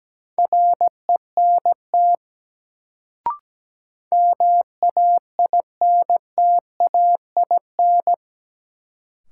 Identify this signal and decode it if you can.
Morse code, heard as a keyed tone, about 17 wpm, 700 Hz. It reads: RENT MAINTAIN
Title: Sentences from Top 1000 Words - Review Part 5 - New Word Drill - 17wpm